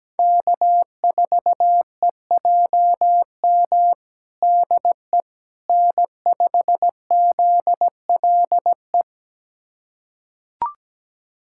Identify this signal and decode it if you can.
Morse code, heard as a keyed tone, about 17 wpm, 700 Hz. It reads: K4EJM DE N5ZLE
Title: USA Callsign DE Callsign - 5 Characters - 17wpm